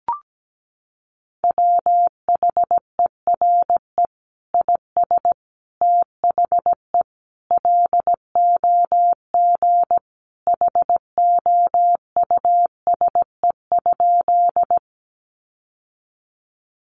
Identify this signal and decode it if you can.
Morse code, heard as a keyed tone, about 17 wpm, 700 Hz. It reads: WHERE IS THE LOG HOUSE?
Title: Sentences from Top 1400 Words 17wpm